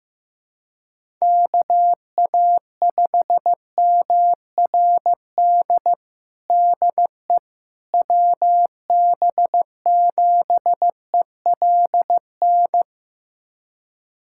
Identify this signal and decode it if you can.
Morse code, heard as a keyed tone, about 15 wpm, 700 Hz. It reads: KA5MRD DE WB7ELN
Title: USA Callsign DE Callsign 15wpm